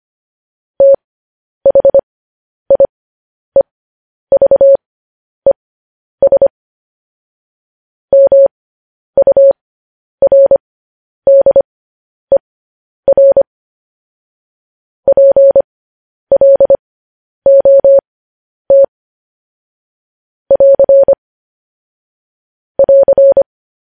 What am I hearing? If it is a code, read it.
Morse code, heard as a keyed tone, about 25 wpm, spaced as for 10 wpm, 550 Hz. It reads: THIEVES MURDER PLOT <AR> <AR>
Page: news-headlines-in-morse-code-at-10-wpm